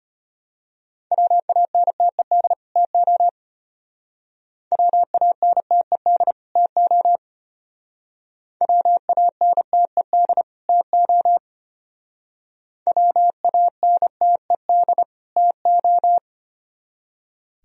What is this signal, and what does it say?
Morse code, heard as a keyed tone, about 32 wpm, 700 Hz. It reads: WANTED TO WANTED TO WANTED TO WANTED TO